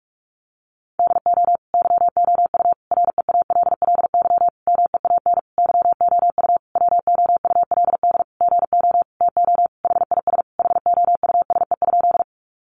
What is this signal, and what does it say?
Morse code, heard as a keyed tone, about 44 wpm, 700 Hz. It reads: DO YOU REALLY MEAN YOU WOULD GO TO HIS HOUSE?
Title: Sentences from Top 600 Words 45wpm